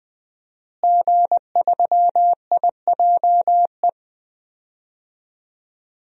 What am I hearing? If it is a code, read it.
Morse code, heard as a keyed tone, about 20 wpm, 700 Hz. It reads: G3IJE
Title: DXCC-100 Call Signs - 5 Characters - Repeated 2x - 20wpm